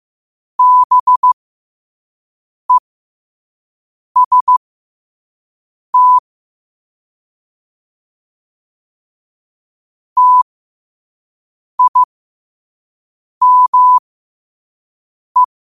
Morse code, heard as a keyed tone, about 15 wpm, spaced as for 5 wpm, 1000 Hz. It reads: BEST TIME
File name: qotdfiles / 5wpm